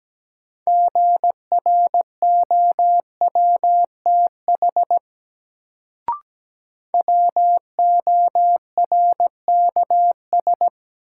Morse code, heard as a keyed tone, about 17 wpm, 700 Hz. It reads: GROWTH WORKS
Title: Sentences from Top 1000 Words - Review Part 5 - New Word Drill - 17wpm